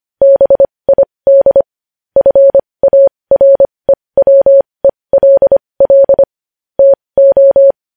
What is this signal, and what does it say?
Morse code, heard as a keyed tone, about 25 wpm, spaced as for 20 wpm, 550 Hz. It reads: BID FAREWELL TO
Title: News Headlines at 20